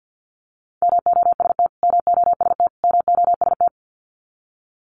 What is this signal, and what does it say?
Morse code, heard as a keyed tone, about 48 wpm, 700 Hz. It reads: MOST MOST MOST
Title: Top 300 Words Repeated 50wpm